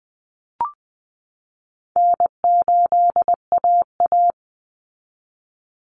Morse code, heard as a keyed tone, about 20 wpm, 700 Hz. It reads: N8AA
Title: Easy Call Signs Repeated 3x - 20wpm